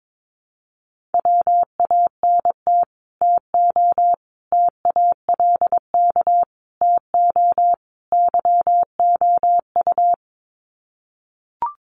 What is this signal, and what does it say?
Morse code, heard as a keyed tone, about 22 wpm, 700 Hz. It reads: WANT TO TALK TO YOU
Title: Sets of 5 Words Encore 22wpm